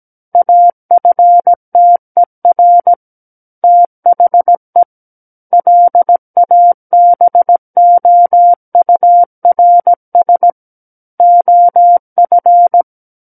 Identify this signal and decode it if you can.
Morse code, heard as a keyed tone, about 17 wpm, 700 Hz. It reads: AFTER THE LABOURS OF
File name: Book One / 08-Chapter 8 - 17 WPM